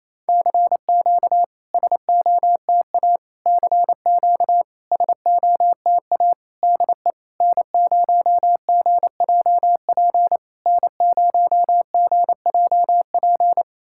Morse code, heard as a keyed tone, about 27 wpm, 690 Hz. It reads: CQ SOTA CQ SOTA DE N0GJP N0GJP
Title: SOTA QSO Practice 28wpm